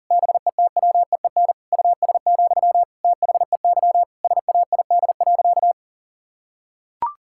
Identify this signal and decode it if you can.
Morse code, heard as a keyed tone, about 40 wpm, 700 Hz. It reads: BETWEEN US, THEY SAID.
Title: Sentences from Top 1600 Words 40wpm